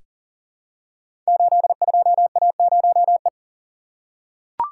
Morse code, heard as a keyed tone, about 42 wpm, 690 Hz. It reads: 82A0E
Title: Code Groups - 5 Letter:Numbers - 40wpm